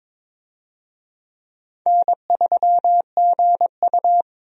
Morse code, heard as a keyed tone, about 22 wpm, 700 Hz. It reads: N3GU
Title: USA Call Signs - 4 Characters - 22wpm